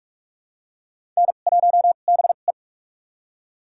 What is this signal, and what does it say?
Morse code, heard as a keyed tone, about 42 wpm, spaced as for 31 wpm, 700 Hz. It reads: N1BE
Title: USA Call Signs - 2x Character Spacing - 45wpm